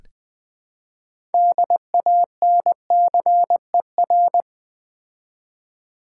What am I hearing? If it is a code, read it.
Morse code, heard as a keyed tone, about 20 wpm, 700 Hz. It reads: DANCER
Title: Build A Word - 3 Letters At A Time - 6-Letter Words - 20wpm